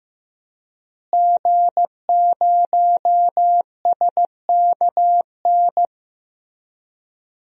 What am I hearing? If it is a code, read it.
Morse code, heard as a keyed tone, about 15 wpm, 700 Hz. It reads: G0SKN
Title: DXCC-100 Call Signs - 5 Characters - Repeated 2x - 15wpm